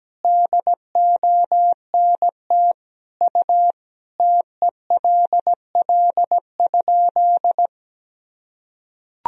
Morse code, heard as a keyed tone, about 17 wpm, 700 Hz. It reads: DONT U TELL?
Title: Sentences from Top 300 Words and Abbreviations 17wpm